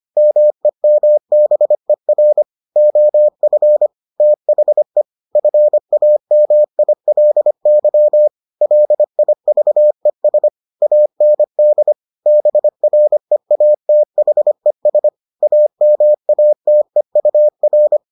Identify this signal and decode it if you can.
Morse code, heard as a keyed tone, about 25 wpm, 600 Hz. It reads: MEMBER OF THE FAMILY LIVES AND BREATHES AMATEUR